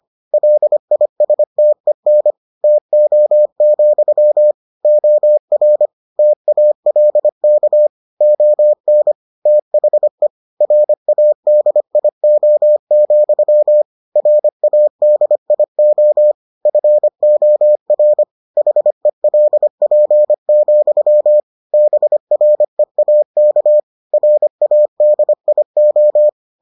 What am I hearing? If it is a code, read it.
Morse code, heard as a keyed tone, about 25 wpm, 600 Hz. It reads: LISTEN TO, OR TALK ON THE RADIO, RADIO FOR HELP, BREAK RADIO